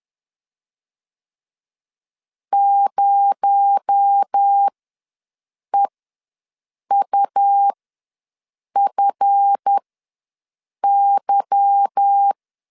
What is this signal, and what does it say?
Morse code, heard as a keyed tone, about 10 wpm, spaced as for 6 wpm, 800 Hz. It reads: 0EUFY